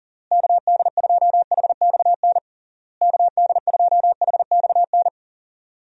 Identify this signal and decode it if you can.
Morse code, heard as a keyed tone, about 40 wpm, 700 Hz. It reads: KD2HXN KD2HXN
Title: USA Call Signs - 6 Characters - Repeated 3x - 40wpm